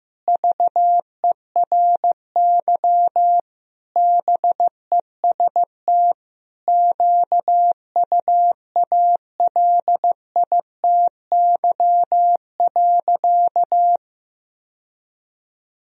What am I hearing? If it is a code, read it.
Morse code, heard as a keyed tone, about 15 wpm, 700 Hz. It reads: VERY BEST QUALITY.
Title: Sentences from Top 1000 Words - Review Part 4 - 15wpm